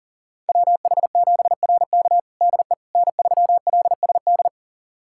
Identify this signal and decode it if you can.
Morse code, heard as a keyed tone, about 40 wpm, 700 Hz. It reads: WH7RK DE N3LSD
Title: USA Callsign DE Callsign - 5 Characters - 40wpm